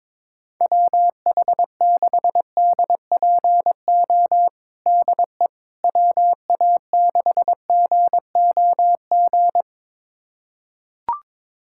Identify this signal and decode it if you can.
Morse code, heard as a keyed tone, about 22 wpm, 700 Hz. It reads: WH6DPO DE WA6GOG
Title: USA Callsign DE Callsign 22wpm